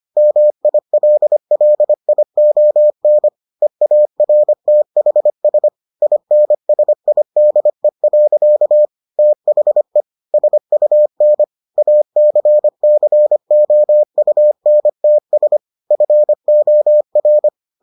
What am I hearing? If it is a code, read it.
Morse code, heard as a keyed tone, about 25 wpm, 600 Hz. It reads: MILLION EARTHS INSIDE. THE SUN ACCOUNTS FOR